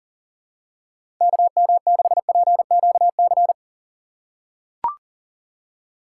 Morse code, heard as a keyed tone, about 39 wpm, 700 Hz. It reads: KM6PQC
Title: USA Call Signs - 6 Characters - 3x Character Spacing - 40wpm